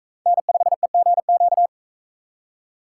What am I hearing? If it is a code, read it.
Morse code, heard as a keyed tone, about 41 wpm, 700 Hz. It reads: N5EGQ